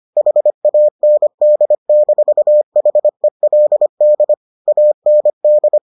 Morse code, heard as a keyed tone, about 25 wpm, 600 Hz. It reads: HAND-HELD AND